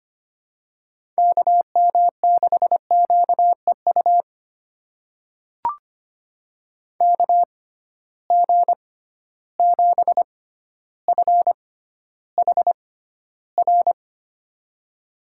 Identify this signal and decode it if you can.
Morse code, heard as a keyed tone, about 25 wpm, spaced as for 13 wpm, 700 Hz. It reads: KM6QEU KG7FHR
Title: USA Call Signs - 6 Characters - 6x Character Spacing - 25wpm